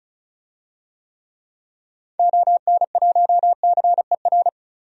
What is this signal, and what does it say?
Morse code, heard as a keyed tone, about 36 wpm, 700 Hz. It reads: ON1CER
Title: DXCC-100 Call Signs - 6 Characters - 35wpm